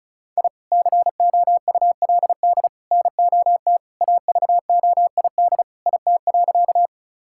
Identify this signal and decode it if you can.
Morse code, heard as a keyed tone, about 36 wpm, 690 Hz. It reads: I COULD NOT AVOID IT.
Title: Sentences from Top 1000 Words - Review Part 6 - 35wpm